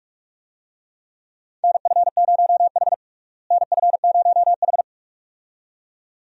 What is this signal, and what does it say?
Morse code, heard as a keyed tone, about 46 wpm, 700 Hz. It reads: NF0H NF0H
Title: USA Call Signs - 4 Characters - Repeated 2x - 45wpm